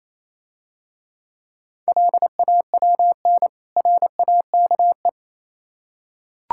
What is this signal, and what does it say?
Morse code, heard as a keyed tone, about 26 wpm, 700 Hz. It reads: LAWN RAKE
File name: Easy Words - Speed-Racing with Hints - 22wpm